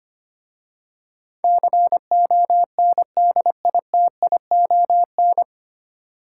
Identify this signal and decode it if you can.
Morse code, heard as a keyed tone, about 25 wpm, 700 Hz. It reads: CONDITION
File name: practice / 25wpm / 9 Letter Words - First 3 Letters Given - 25wpm